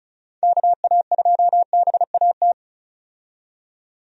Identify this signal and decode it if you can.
Morse code, heard as a keyed tone, about 36 wpm, 700 Hz. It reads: KA2BAT
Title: USA Call Signs - 6 Characters - 35wpm